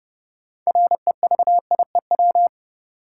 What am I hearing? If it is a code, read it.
Morse code, heard as a keyed tone, about 31 wpm, 700 Hz. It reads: REVIEW
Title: Sentences from Top 900 Words - New Word Drill - 30wpm